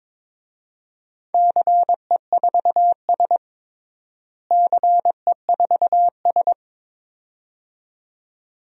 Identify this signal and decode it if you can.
Morse code, heard as a keyed tone, about 22 wpm, 700 Hz. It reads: CE4S CE4S
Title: DXCC-100 Call Signs - 4 Characters - Repeated 2x - 22wpm